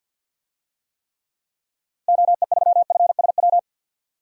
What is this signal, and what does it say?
Morse code, heard as a keyed tone, about 52 wpm, 700 Hz. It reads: OE3FSW
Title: DXCC-100 Call Signs - 6 Characters - Repeated 3x - 50wpm